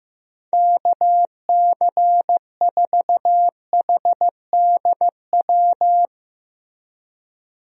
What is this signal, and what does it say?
Morse code, heard as a keyed tone, about 15 wpm, 700 Hz. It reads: KC4HDW
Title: USA Call Signs - 6 Characters - Repeated 2x - 15wpm